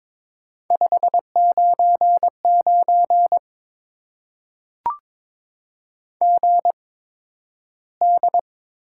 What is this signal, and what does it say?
Morse code, heard as a keyed tone, about 22 wpm, spaced as for 12 wpm, 700 Hz. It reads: 599 GD
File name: Top 90 QSO Elements - 8x Character Spacing - 22wpm